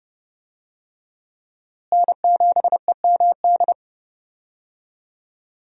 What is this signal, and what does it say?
Morse code, heard as a keyed tone, about 30 wpm, 700 Hz. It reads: N7EMD
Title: USA Call Signs - 5 Characters - Repeated 3x - 30wpm